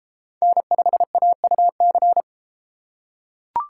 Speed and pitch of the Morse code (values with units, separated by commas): 29 wpm, 700 Hz